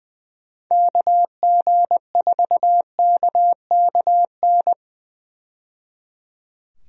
Morse code, heard as a keyed tone, about 20 wpm, 700 Hz. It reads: KG4KKN
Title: USA Call Signs - 6 Characters - 20wpm